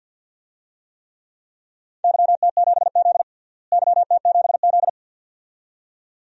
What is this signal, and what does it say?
Morse code, heard as a keyed tone, about 52 wpm, 700 Hz. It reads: YT7Z YT7Z
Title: DXCC-100 Call Signs - 4 Characters - Repeated 2x - 50wpm